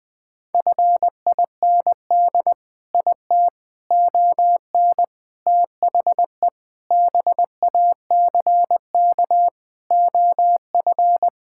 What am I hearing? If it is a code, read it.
Morse code, heard as a keyed tone, about 20 wpm, 700 Hz. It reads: FIND IT ON THE BACK OF
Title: Sentences with 4-Word Phrases - Spoken Beginning - 20wpm